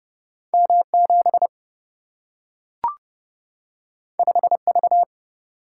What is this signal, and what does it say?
Morse code, heard as a keyed tone, about 30 wpm, 700 Hz. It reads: M 7 5 V
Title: Single-Letter Single-Number 30wpm